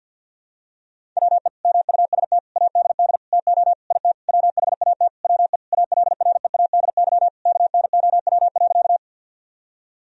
Morse code, heard as a keyed tone, about 52 wpm, 700 Hz. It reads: WE MUST ADD TO IT WHAT WE ALREADY KNOW.
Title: Sentences from Top 1000 Words - Review Part 1 - 50wpm